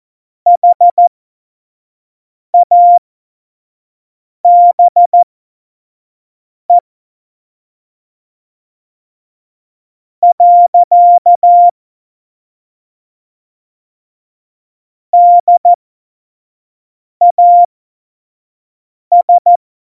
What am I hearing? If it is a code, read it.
Morse code, heard as a keyed tone, about 14 wpm, spaced as for 4 wpm, 700 Hz. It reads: HABE. DAS
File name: Morsekurs / Witze / Witz 04. Juni 2019, 25 F70